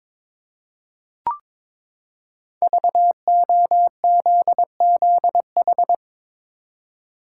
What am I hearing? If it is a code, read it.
Morse code, heard as a keyed tone, about 23 wpm, 700 Hz. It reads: VOZZH
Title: Code Groups - 5 Letter:Numbers - 22wpm